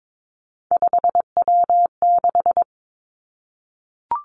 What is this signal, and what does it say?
Morse code, heard as a keyed tone, about 22 wpm, 700 Hz. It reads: 5W6
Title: Three Challenging Letter-Number Combinations 22wpm